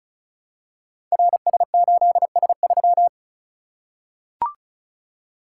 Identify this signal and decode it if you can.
Morse code, heard as a keyed tone, about 36 wpm, 700 Hz. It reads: RS8S3
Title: Code Groups - 5 Letter:Numbers - 35wpm